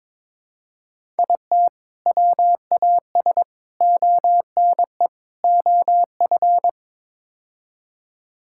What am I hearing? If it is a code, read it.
Morse code, heard as a keyed tone, about 22 wpm, 700 Hz. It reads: IT WAS ONE OF
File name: practice / 22wpm / Build A Phrase - 5-Word Phrases - 22wpm